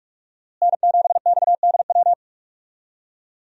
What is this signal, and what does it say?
Morse code, heard as a keyed tone, about 41 wpm, 700 Hz. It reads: N7XDW
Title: USA Call Signs - 5 Characters - Repeated 3x - 45wpm